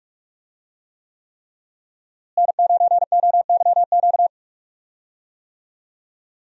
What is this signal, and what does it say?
Morse code, heard as a keyed tone, about 46 wpm, 700 Hz. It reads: N9OYQ
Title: USA Call Signs - 5 Characters - Repeated 3x - 45wpm